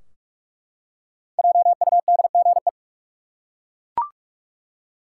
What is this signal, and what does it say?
Morse code, heard as a keyed tone, about 41 wpm, 700 Hz. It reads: JUDGE
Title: Five Letter Words - 45wpm